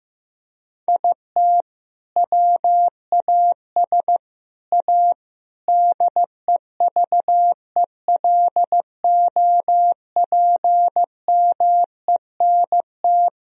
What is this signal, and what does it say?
Morse code, heard as a keyed tone, about 15 wpm, 700 Hz. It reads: IT WAS A DEVELOPMENT
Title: Sentences from Top 1000 Words - Review Part 4 - 15wpm